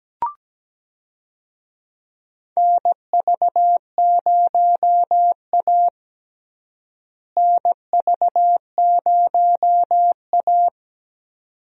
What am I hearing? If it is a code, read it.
Morse code, heard as a keyed tone, about 17 wpm, 700 Hz. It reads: NV0A NV0A
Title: USA Call Signs - 4 Characters - Repeated 2x - 17wpm